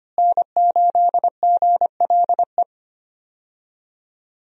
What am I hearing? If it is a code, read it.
Morse code, heard as a keyed tone, about 25 wpm, 700 Hz. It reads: N8GLE
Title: USA Call Signs - 5 Characters - Repeated 2x - 25wpm